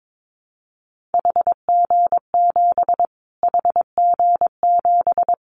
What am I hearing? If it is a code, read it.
Morse code, heard as a keyed tone, about 22 wpm, 700 Hz. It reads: HG7 HG7
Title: Callsign Prefixes for Top 25 DXCC - Repeated 3x - 22wpm